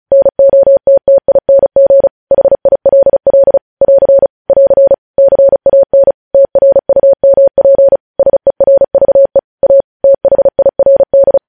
Morse code, heard as a keyed tone, about 36 wpm, 550 Hz. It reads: NOTTING HILL <AR> <AR> CAN TRUMP SERVE A THIRD
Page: News Headlines in Morse Code at 35 WPM